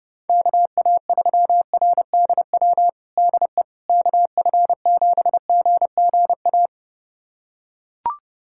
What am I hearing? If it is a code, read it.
Morse code, heard as a keyed tone, about 30 wpm, 700 Hz. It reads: KA3RDW DE KF7GGA
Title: USA Callsign DE Callsign 30wpm